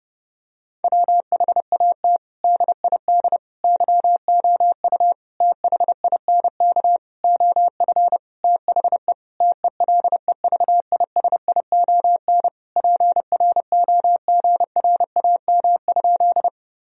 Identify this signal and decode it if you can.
Morse code, heard as a keyed tone, about 30 wpm, 700 Hz. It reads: WHAT DID YOU THINK OF THE TELEVISION PROGRAM?
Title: Sentences from Top 1000 Words - Review Part 4 - 30wpm